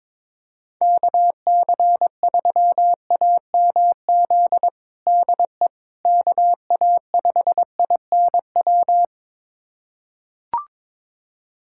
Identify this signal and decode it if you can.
Morse code, heard as a keyed tone, about 22 wpm, 700 Hz. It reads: KC3AMZ DE KA5INW
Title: USA Callsign DE Callsign - 6 Characters - 22wpm